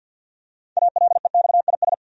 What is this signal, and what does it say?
Morse code, heard as a keyed tone, about 50 wpm, 700 Hz. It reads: ALEXIS
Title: USA Common Names - 50wpm